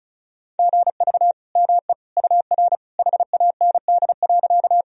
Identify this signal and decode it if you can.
Morse code, heard as a keyed tone, about 36 wpm, 700 Hz. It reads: GV ME UR HAND.
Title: Sentences from Top 200 Words and Abbreviations 35wpm